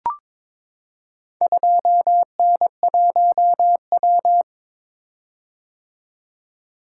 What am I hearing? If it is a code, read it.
Morse code, heard as a keyed tone, about 22 wpm, 700 Hz. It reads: 2N1W
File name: practice / 22wpm / Four Letter-Number Combinations - 22wpm